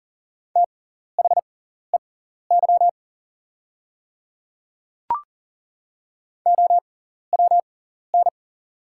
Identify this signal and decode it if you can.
Morse code, heard as a keyed tone, about 42 wpm, spaced as for 14 wpm, 690 Hz. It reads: THEY OWN
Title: 6x Character Spacing - Top 500 Words - 40wpm